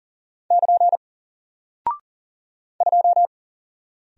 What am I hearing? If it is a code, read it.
Morse code, heard as a keyed tone, about 40 wpm, 700 Hz. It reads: <KN> 2
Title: Single Character (Letter:Number:Punctuation:Prosign) Alt 40wpm